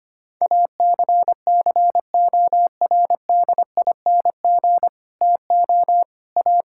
Morse code, heard as a keyed tone, about 25 wpm, 700 Hz. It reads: ACCORDING TO A
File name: Sets of 9 Words - 25wpm